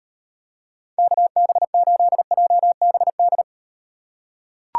Morse code, heard as a keyed tone, about 38 wpm, 700 Hz. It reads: KB8JBD